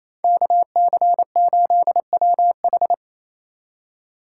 Morse code, heard as a keyed tone, about 28 wpm, 700 Hz. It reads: KC8WH
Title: K1USN SST QSO Practice - 28wpm